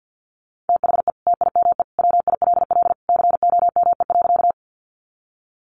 Morse code, heard as a keyed tone, about 50 wpm, 700 Hz. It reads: THE TIME WILL COME.
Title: Sentences from Top 100 Words 50wpm